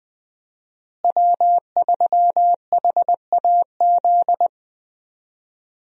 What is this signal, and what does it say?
Morse code, heard as a keyed tone, about 20 wpm, 700 Hz. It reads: W3HAZ
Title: USA Call Signs - 5 Characters - Repeated 3x - 20wpm